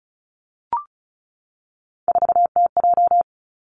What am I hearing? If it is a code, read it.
Morse code, heard as a keyed tone, about 36 wpm, 700 Hz. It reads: 4TJ
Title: Three Challenging Letter-Number Combinations 35wpm